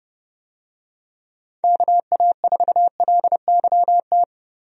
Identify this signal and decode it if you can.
Morse code, heard as a keyed tone, about 32 wpm, 700 Hz. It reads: KA4LYT
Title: USA Call Signs - 6 Characters - Repeated 3x - 30wpm